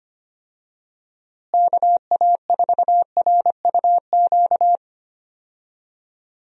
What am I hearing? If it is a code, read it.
Morse code, heard as a keyed tone, about 25 wpm, 700 Hz. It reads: KA4RUQ
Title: USA Call Signs - 6 Characters - Repeated 2x - 25wpm